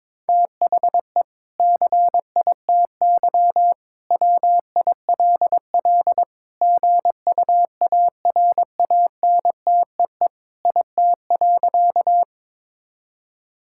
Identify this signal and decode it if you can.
Morse code, heard as a keyed tone, about 22 wpm, 700 Hz. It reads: THE CITY WILL GUARANTEE IT.
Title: Sentences from Top 1600 Words 22wpm